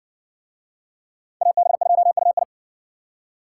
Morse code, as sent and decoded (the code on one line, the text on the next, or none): .- -... ..--- ..- ..
AB2UI